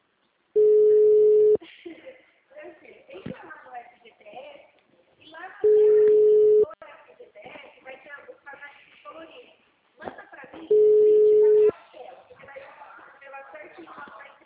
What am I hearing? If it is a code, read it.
Ringback tone (Europe, 425 Hz), with speech.